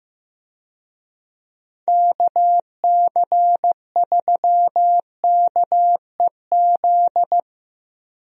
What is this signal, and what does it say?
Morse code, heard as a keyed tone, about 15 wpm, 700 Hz. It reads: KC3KEZ
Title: USA Call Signs - 6 Characters - Repeated 3x - 15wpm